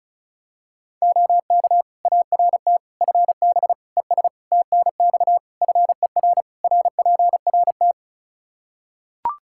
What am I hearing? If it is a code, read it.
Morse code, heard as a keyed tone, about 36 wpm, 700 Hz. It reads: OK ART FB ES TNX FER RPRT
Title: Practice QSO - Exchange 2 - Info - 35wpm